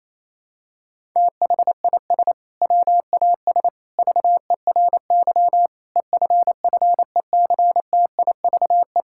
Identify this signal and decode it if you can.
Morse code, heard as a keyed tone, about 28 wpm, 700 Hz. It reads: THIS WAS VERY EFFECTIVE
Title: Sentences from Top 1000 Words - Review Part 6 - 28wpm